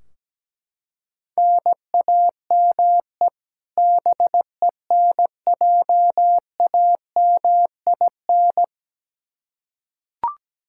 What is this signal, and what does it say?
Morse code, heard as a keyed tone, about 17 wpm, 700 Hz. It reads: NAME BENJAMIN
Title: K1USN SST QSO Practice 17wpm